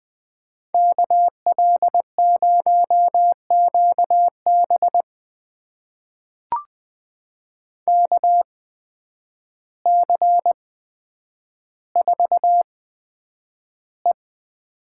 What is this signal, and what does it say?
Morse code, heard as a keyed tone, about 20 wpm, spaced as for 9 wpm, 700 Hz. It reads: KL0QB KC4E
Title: USA Call Signs - 8x Character Spacing - 20wpm